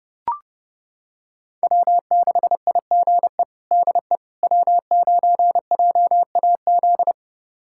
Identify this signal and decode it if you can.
Morse code, heard as a keyed tone, about 30 wpm, 700 Hz. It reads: W6IGE DE W9JAZ
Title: USA Callsign DE Callsign - 5 Characters - 30wpm